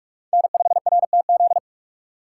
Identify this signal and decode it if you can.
Morse code, heard as a keyed tone, about 47 wpm, 700 Hz. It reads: N5RTZ
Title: USA Call Signs - 4x Character Spacing - 45wpm